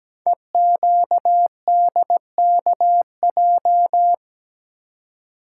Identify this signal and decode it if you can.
Morse code, heard as a keyed tone, about 17 wpm, 700 Hz. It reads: EQDKJ
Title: Code Groups - 5 Letters - 17wpm